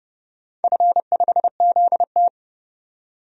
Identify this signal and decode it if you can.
Morse code, heard as a keyed tone, about 32 wpm, 700 Hz. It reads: F5ZT
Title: DXCC-100 Call Signs - 4 Characters - 30wpm